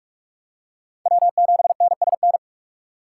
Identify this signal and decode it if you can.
Morse code, heard as a keyed tone, about 46 wpm, 700 Hz. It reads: W7NSN
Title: USA Call Signs - 5 Characters - 3x Character Spacing - 45wpm